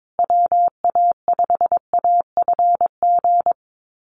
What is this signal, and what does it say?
Morse code, heard as a keyed tone, about 22 wpm, 700 Hz. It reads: WA5AFG
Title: USA Call Signs Repeated 2x 22wpm